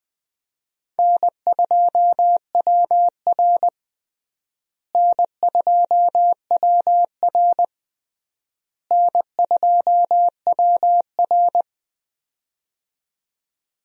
Morse code, heard as a keyed tone, about 20 wpm, 700 Hz. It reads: N2WR N2WR N2WR
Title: USA Call Signs - 4 Characters - Repeated 3x - 20wpm